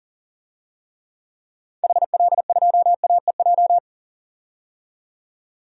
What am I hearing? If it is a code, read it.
Morse code, heard as a keyed tone, about 40 wpm, 700 Hz. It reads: HL2AEJ
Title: DXCC-100 Call Signs - 6 Characters - 40wpm